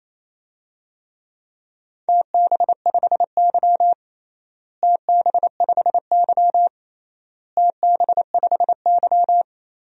Morse code, heard as a keyed tone, about 28 wpm, 700 Hz. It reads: TB5Y TB5Y TB5Y
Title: DXCC-100 Call Signs - 4 Characters - Repeated 3x - 28wpm